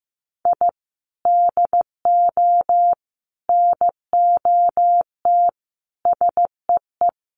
Morse code, heard as a keyed tone, about 15 wpm, 700 Hz. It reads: I DO NOT SEE
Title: Sets of 5 Words 15wpm